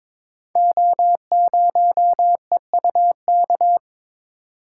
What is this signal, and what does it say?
Morse code, heard as a keyed tone, about 22 wpm, 700 Hz. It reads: O0EUK
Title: Code Groups - 5 Letter:Numbers - 22wpm